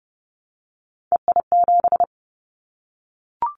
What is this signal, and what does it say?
Morse code, heard as a keyed tone, about 30 wpm, 700 Hz. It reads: EI7
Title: Callsign Prefixes for Top 50 DXCC - 30wpm